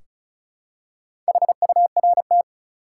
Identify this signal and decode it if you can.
Morse code, heard as a keyed tone, about 36 wpm, 700 Hz. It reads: HURT
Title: Four Letter Words - 35wpm